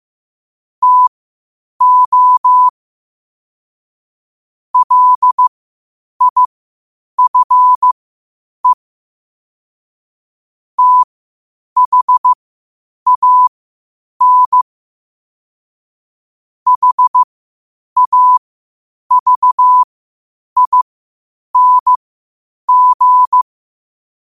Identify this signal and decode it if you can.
Morse code, heard as a keyed tone, about 15 wpm, spaced as for 8 wpm, 1000 Hz. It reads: TO LIFE THAN HAVING
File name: qotdfiles / 8wpm